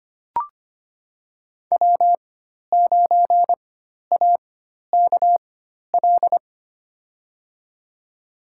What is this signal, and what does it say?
Morse code, heard as a keyed tone, about 25 wpm, spaced as for 12 wpm, 700 Hz. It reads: W9AKL
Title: USA Call Signs - 5 Characters - 4x Character Spacing - 25wpm